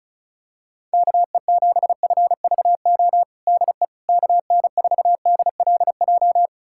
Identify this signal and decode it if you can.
Morse code, heard as a keyed tone, about 34 wpm, 700 Hz. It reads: KE7FVO DE KN4DLJ
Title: USA Callsign DE Callsign - 6 Characters - 35wpm